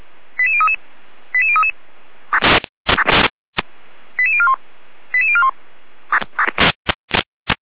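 Five-tone selective call (ZVEI), heard as quick runs of five tones, tones of about 70 ms. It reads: 88033 88033 88041 88041
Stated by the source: ZVEI five-tone call received by SDR.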